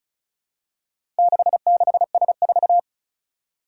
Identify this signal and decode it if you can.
Morse code, heard as a keyed tone, about 36 wpm, 700 Hz. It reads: 66S4
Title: Four Challenging Letter-Number Combinations 35wpm